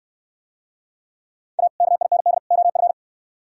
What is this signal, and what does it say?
Morse code, heard as a keyed tone, about 66 wpm, 700 Hz. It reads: A KIND OF